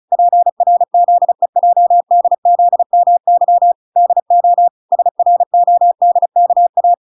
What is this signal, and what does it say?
Morse code, heard as a keyed tone, about 35 wpm, 700 Hz. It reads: PRZEJDZMY DO SRODKA